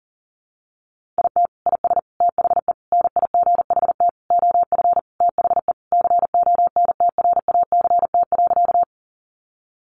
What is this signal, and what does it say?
Morse code, heard as a keyed tone, about 40 wpm, 700 Hz. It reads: IT IS THE NIGHT OF THE CONTRACT.
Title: Sentences from Top 1400 Words 40wpm